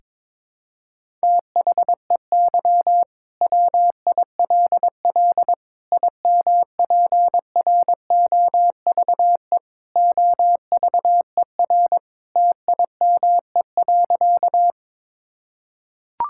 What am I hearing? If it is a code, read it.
Morse code, heard as a keyed tone, about 22 wpm, 700 Hz. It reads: THEY WILL IMPROVE OVER TIME.
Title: Sentences from Top 1000 Words - Review Part 5 - 22wpm